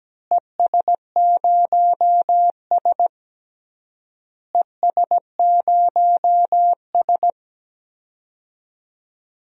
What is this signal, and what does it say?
Morse code, heard as a keyed tone, about 17 wpm, 700 Hz. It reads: ES0S ES0S
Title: DXCC-100 Call Signs - 4 Characters - Repeated 3x - 17wpm